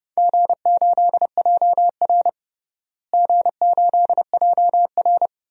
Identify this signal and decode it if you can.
Morse code, heard as a keyed tone, about 30 wpm, 700 Hz. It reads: G8JR G8JR
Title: DXCC-100 Call Signs - 4 Characters - Repeated 2x - 30wpm